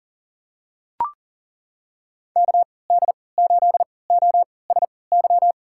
Morse code, heard as a keyed tone, about 40 wpm, spaced as for 23 wpm, 700 Hz. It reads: KD8OSY
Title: USA Call Signs - 6 Characters - 3x Character Spacing - 40wpm